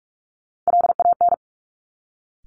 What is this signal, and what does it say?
Morse code, heard as a keyed tone, about 46 wpm, 690 Hz. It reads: LAN
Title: Top Three Letter Combinations in Words 45wpm